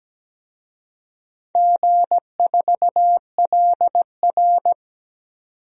Morse code, heard as a keyed tone, about 17 wpm, 700 Hz. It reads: G4LR
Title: DXCC-100 Call Signs - 4 Characters - 17wpm